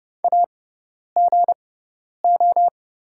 Morse code, heard as a keyed tone, about 30 wpm, spaced as for 10 wpm, 690 Hz. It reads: AGO
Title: 6x Character Spacing - Top 500 Words - 30wpm